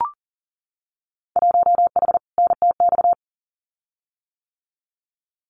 Morse code, heard as a keyed tone, about 40 wpm, 700 Hz. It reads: 1H NTX
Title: Winter Field Day Exchanges 40wpm